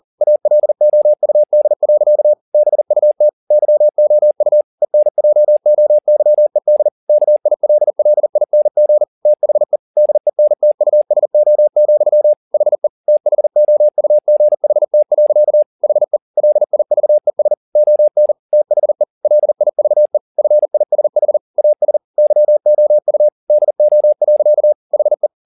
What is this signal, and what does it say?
Morse code, heard as a keyed tone, about 40 wpm, 600 Hz. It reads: ALOUD. BUT YOU ENJOYED KILLING THE DENTUSO, HE THOUGHT. HE LIVES ON THE LIVE FISH AS YOU DO. HE
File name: the old man and the sea 2 40wpm